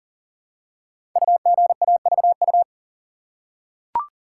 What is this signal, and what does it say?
Morse code, heard as a keyed tone, about 42 wpm, 690 Hz. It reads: UGAVU
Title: Code Groups - 5 Letter:Numbers - 40wpm